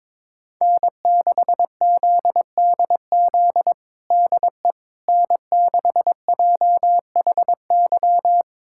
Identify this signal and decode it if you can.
Morse code, heard as a keyed tone, about 22 wpm, 700 Hz. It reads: N6ZDZ DE N6JHY
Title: USA Callsign DE Callsign - 5 Characters - 22wpm